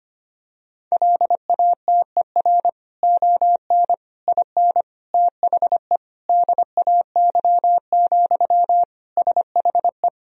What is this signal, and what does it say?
Morse code, heard as a keyed tone, about 25 wpm, 700 Hz. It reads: LATER ON IN THE DAY, SHE
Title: Sentences from Top 1000 Words - Review Part 4 - 25wpm